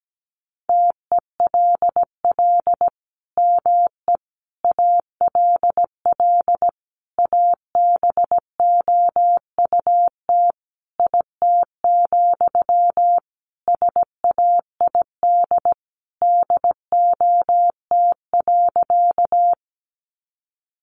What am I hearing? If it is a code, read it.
Morse code, heard as a keyed tone, about 17 wpm, 700 Hz. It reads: TELL ME ALL ABOUT IT, SAID DOT.
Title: Sentences from Top 1500 Words 17wpm